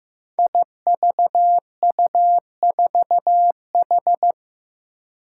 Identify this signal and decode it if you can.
Morse code, heard as a keyed tone, about 15 wpm, 700 Hz. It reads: IVU4H
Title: Five Challenging Letter-Number Combinations 15wpm